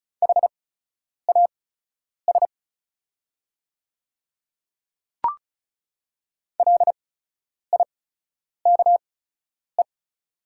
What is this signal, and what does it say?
Morse code, heard as a keyed tone, about 36 wpm, spaced as for 10 wpm, 690 Hz. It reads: HAS LIKE
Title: 8x Character Spacing - Top 200 Words - 35wpm